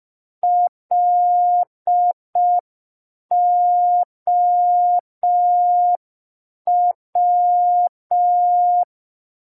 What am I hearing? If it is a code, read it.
Morse code, heard as a keyed tone, about 5 wpm, 710 Hz. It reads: LOW
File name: g3zpd / Morse code P4